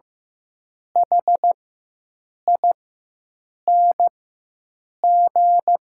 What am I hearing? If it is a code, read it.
Morse code, heard as a keyed tone, about 15 wpm, spaced as for 7 wpm, 700 Hz. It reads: HING
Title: 4x Character Spacing - Top 300 Words - 15wpm